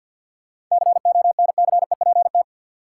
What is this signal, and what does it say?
Morse code, heard as a keyed tone, about 53 wpm, 700 Hz. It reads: CONCEPT
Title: Sentences from Top 1100 Words - New Word Drill - 50wpm